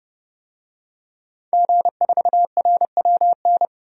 Morse code, heard as a keyed tone, about 30 wpm, 700 Hz. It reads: G4RWN
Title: DXCC-100 Call Signs - 5 Characters - 30wpm